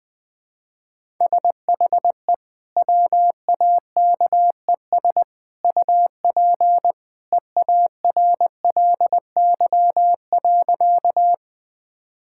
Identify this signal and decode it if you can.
Morse code, heard as a keyed tone, about 21 wpm, 700 Hz. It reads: SHE WAKES UP EARLY.
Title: Easy Sentences with Hints - 20wpm